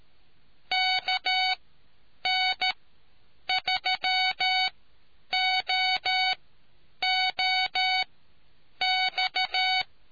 Morse code, heard as a keyed tone, about 13 wpm, spaced as for 8 wpm, 750 Hz. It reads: KN3OOX